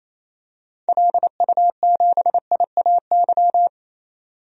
Morse code, heard as a keyed tone, about 27 wpm, 700 Hz. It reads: LU7IAY
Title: DXCC-100 Call Signs - 6 Characters - 28wpm